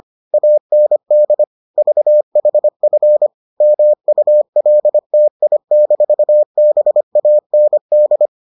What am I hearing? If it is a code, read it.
Morse code, heard as a keyed tone, about 25 wpm, 600 Hz. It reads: AND VHF MULTI-BAND